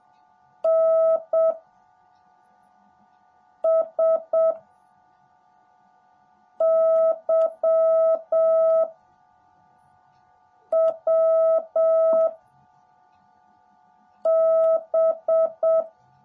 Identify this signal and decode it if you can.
Morse code, heard as a keyed tone, about 7 wpm, 650 Hz. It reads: N S Y W B
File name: jh8cbh / 2021cw / cw18a